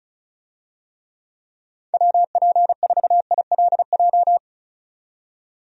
Morse code, heard as a keyed tone, about 34 wpm, 700 Hz. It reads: WP4ILJ
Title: USA Call Signs - 6 Characters - 35wpm